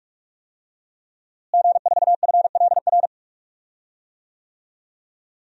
Morse code, heard as a keyed tone, about 41 wpm, 700 Hz. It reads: G4FLR
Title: DXCC-100 Call Signs - 5 Characters - Repeated 3x - 45wpm